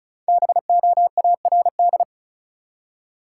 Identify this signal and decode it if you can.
Morse code, heard as a keyed tone, about 36 wpm, 700 Hz. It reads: BOARD
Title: Five Letter Words - 35wpm